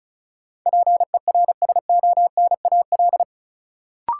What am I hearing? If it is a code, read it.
Morse code, heard as a keyed tone, about 36 wpm, 700 Hz. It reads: PERSONAL
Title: Sentences from Top 700 Words - New Word Drill - 35wpm